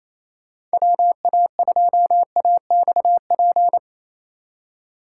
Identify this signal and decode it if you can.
Morse code, heard as a keyed tone, about 34 wpm, 700 Hz. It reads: WA2AXP
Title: Speed Racing - USA Call Signs 28wpm